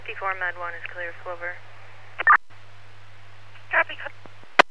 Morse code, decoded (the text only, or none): none